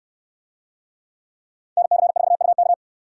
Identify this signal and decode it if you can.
Morse code, heard as a keyed tone, about 69 wpm, 700 Hz. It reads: NC4UK